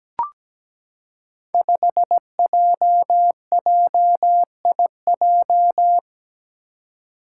Morse code, heard as a keyed tone, about 17 wpm, 700 Hz. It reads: 5JJIJ
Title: Five Challenging Letter-Number Combinations 17wpm